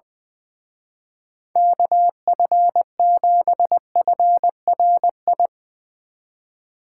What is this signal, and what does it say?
Morse code, heard as a keyed tone, about 22 wpm, 700 Hz. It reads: KF7FRI
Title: Speed Racing - USA 2x3 Call Signs 20wpm